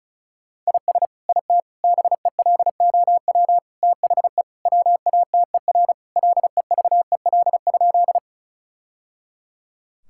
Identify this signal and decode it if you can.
Morse code, heard as a keyed tone, about 36 wpm, 700 Hz. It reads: IS IT BELOW THE WATER LEVEL?
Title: Sentences from Top 1000 Words - Review Part 5 - 35wpm